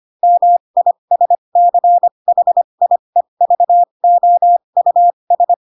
Morse code, heard as a keyed tone, about 25 wpm, spaced as for 22 wpm, 700 Hz. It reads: MISCHIEVOUS